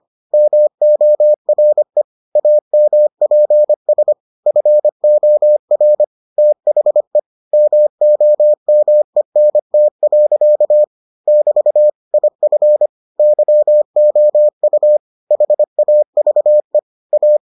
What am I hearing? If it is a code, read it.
Morse code, heard as a keyed tone, about 25 wpm, 600 Hz. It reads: MORE AMPS FOR THE MOMENT. <BT> IF YOU HAVE A